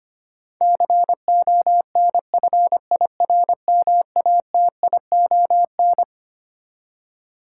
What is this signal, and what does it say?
Morse code, heard as a keyed tone, about 25 wpm, 700 Hz. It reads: CONFIRMATION
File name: Twelve Letter Words - 25wpm